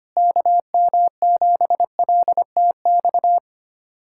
Morse code, heard as a keyed tone, about 25 wpm, 700 Hz. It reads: KM7LTX
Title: USA Callsign DE Callsign 25wpm